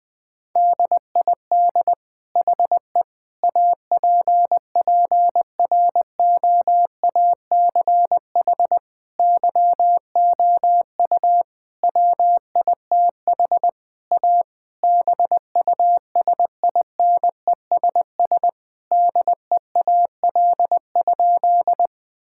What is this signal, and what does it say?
Morse code, heard as a keyed tone, about 20 wpm, 700 Hz. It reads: DID HE APPROACH YOU WITH A BUSINESS DEAL?
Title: Sentences from Top 1000 Words - Review Part 6 - 20wpm